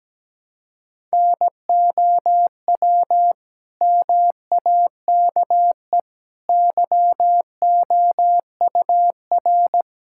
Morse code, heard as a keyed tone, about 17 wpm, 700 Hz. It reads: NOW MAKE YOUR
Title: Sentences from Top 1000 Words - Review Part 6 - 17wpm